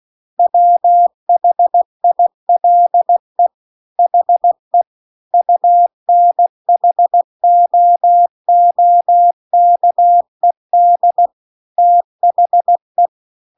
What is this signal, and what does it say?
Morse code, heard as a keyed tone, about 16 wpm, 700 Hz. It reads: WHILE HE UNHOOKED THE
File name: CW-The-Big-Twoheaded-River-2-16WPM